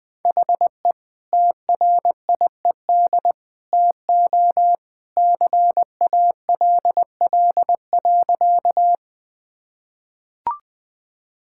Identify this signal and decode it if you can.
Morse code, heard as a keyed tone, about 20 wpm, 700 Hz. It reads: HE TRIED TO CALL.
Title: Sentences with 2-Word Phrases - 20wpm